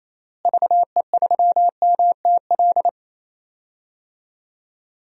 Morse code, heard as a keyed tone, about 28 wpm, 700 Hz. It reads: VE3MTL
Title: DXCC-100 Call Signs - 6 Characters - Repeated 2x - 28wpm